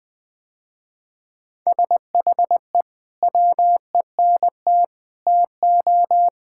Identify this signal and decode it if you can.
Morse code, heard as a keyed tone, about 19 wpm, 700 Hz. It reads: SHE WENT TO
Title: Sentences with 2-Word Phrases - Spoken Beginning - 20wpm